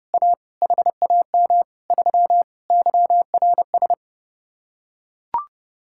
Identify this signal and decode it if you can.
Morse code, heard as a keyed tone, about 30 wpm, 700 Hz. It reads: A HAM 3 YRS